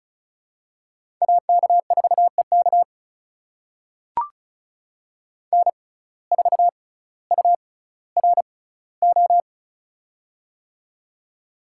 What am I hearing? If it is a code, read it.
Morse code, heard as a keyed tone, about 36 wpm, spaced as for 18 wpm, 700 Hz. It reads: AK4EK N4URO
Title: USA Call Signs - 5 Characters - 6x Character Spacing - 35wpm